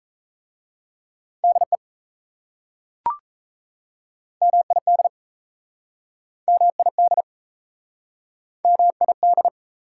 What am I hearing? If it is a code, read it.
Morse code, heard as a keyed tone, about 36 wpm, 700 Hz. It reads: DE MID MID MID